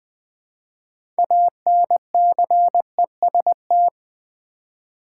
Morse code, heard as a keyed tone, about 20 wpm, 700 Hz. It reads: ANCEST
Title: Build A Word - 2 Letters At A Time - 8-Letter Words - 20wpm